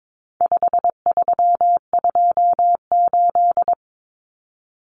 Morse code, heard as a keyed tone, about 22 wpm, 700 Hz. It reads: 5328
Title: Numbers - 4-Digit 22wpm